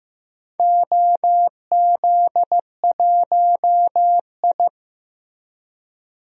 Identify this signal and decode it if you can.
Morse code, heard as a keyed tone, about 15 wpm, 700 Hz. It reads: OZ1I
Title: DXCC-100 Call Signs - 4 Characters - Repeated 2x - 15wpm